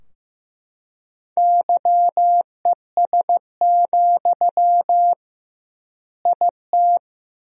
Morse code, heard as a keyed tone, about 15 wpm, 700 Hz. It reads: YES, IT
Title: 2x Word-Spacing - Sentences from Top 500 Words - 15wpm